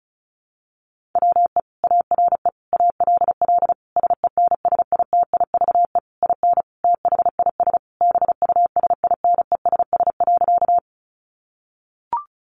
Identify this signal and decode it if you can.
Morse code, heard as a keyed tone, about 34 wpm, 700 Hz. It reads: WE ARE ALL SENSITIVE IN THIS BUSINESS.
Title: Sentences from Top 1400 Words 35wpm